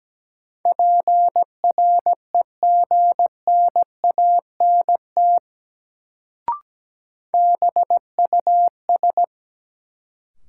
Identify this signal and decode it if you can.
Morse code, heard as a keyed tone, about 17 wpm, 700 Hz. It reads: PREGNANT BUS
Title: Sentences from Top 1000 Words - Review Part 6 - New Word Drill - 17wpm